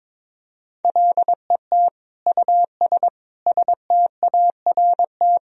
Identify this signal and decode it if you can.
Morse code, heard as a keyed tone, about 22 wpm, 700 Hz. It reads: LET US START
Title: Sentences from Top 1000 Words - Review Part 1 - 22wpm